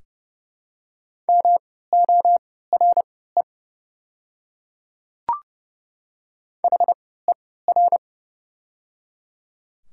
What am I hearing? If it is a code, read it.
Morse code, heard as a keyed tone, about 30 wpm, spaced as for 17 wpm, 700 Hz. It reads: MORE HER
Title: 3x Character Spacing - Top 300 Words - 30wpm